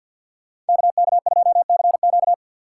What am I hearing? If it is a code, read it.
Morse code, heard as a keyed tone, about 51 wpm, 700 Hz. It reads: KK2CQ
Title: USA Call Signs - 5 Characters - 3x Character Spacing - 50wpm